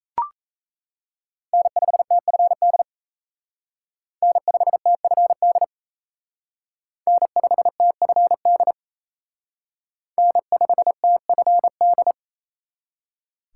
Morse code, heard as a keyed tone, about 32 wpm, 700 Hz. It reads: N5TFD N5TFD N5TFD N5TFD
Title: Speed Racing - USA 1x3 Call Signs 28wpm